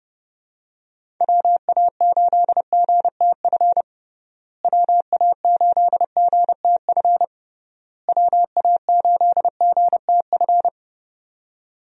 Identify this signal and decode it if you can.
Morse code, heard as a keyed tone, about 30 wpm, 700 Hz. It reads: WA8GTF WA8GTF WA8GTF
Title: USA Call Signs - 6 Characters - Repeated 3x - 30wpm